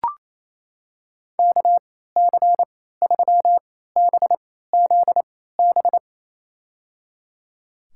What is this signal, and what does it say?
Morse code, heard as a keyed tone, about 28 wpm, spaced as for 16 wpm, 700 Hz. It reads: KC3BZB
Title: USA Call Signs - 6 Characters - 3x Character Spacing - 28wpm